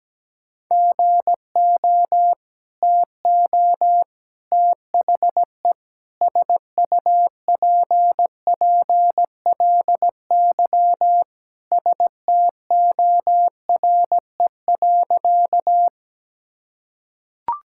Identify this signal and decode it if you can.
Morse code, heard as a keyed tone, about 17 wpm, 700 Hz. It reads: GO TO THE SUPPLY STORE.
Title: Sentences from Top 1000 Words - Review Part 5 - 17wpm